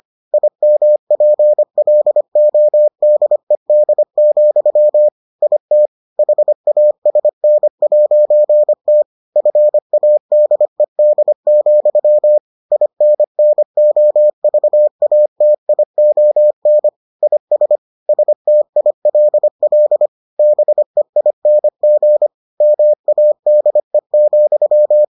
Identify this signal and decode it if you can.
Morse code, heard as a keyed tone, about 25 wpm, 600 Hz. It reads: IMPLODED, IT HASN'T FADED, INNOVATION IS STILL BEING MADE,